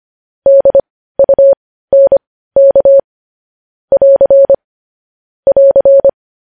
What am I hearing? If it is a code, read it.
Morse code, heard as a keyed tone, about 25 wpm, spaced as for 15 wpm, 550 Hz. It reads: DUNK <AR> <AR>
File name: podcast / News headlines-1x-25x15